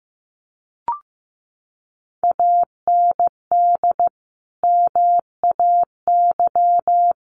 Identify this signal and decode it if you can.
Morse code, heard as a keyed tone, about 15 wpm, 700 Hz. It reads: AND MAY
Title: Sentences from Top 1500 Words 15wpm